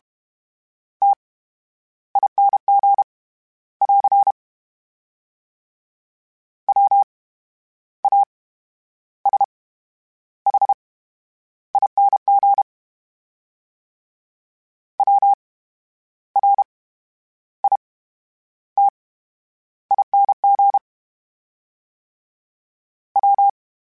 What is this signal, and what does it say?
Morse code, heard as a keyed tone, about 32 wpm, 800 Hz. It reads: T I N G <AR> W A S H I N G W R I T I N G W
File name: Special INGs / Special INGs-mp3s / VerySlowToFaster Ing @30